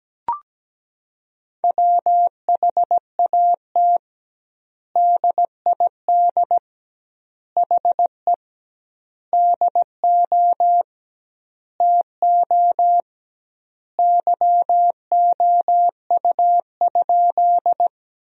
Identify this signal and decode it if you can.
Morse code, heard as a keyed tone, about 17 wpm, 700 Hz. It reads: WHAT DID HE DO TO YOU?
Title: 2x Word-Spacing - Sentences from Top 100 Words - 17wpm